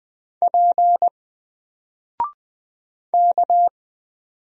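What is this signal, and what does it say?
Morse code, heard as a keyed tone, about 20 wpm, 700 Hz. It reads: P K
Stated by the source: Single letter and number Morse drill at 20 wpm, from N4LCD.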